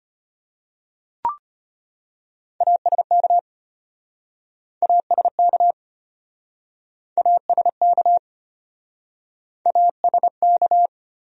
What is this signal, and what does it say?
Morse code, heard as a keyed tone, about 33 wpm, 700 Hz. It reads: ASK ASK ASK ASK